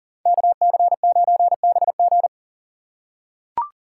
Morse code, heard as a keyed tone, about 40 wpm, 700 Hz. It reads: KC9BG
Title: K1USN SST QSO Practice - 40wpm